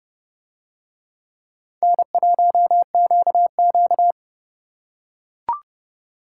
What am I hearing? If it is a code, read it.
Morse code, heard as a keyed tone, about 29 wpm, 700 Hz. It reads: N1QQ